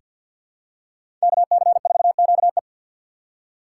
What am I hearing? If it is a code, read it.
Morse code, heard as a keyed tone, about 48 wpm, 700 Hz. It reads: KC4QE
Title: USA Call Signs - 5 Characters - 3x Character Spacing - 50wpm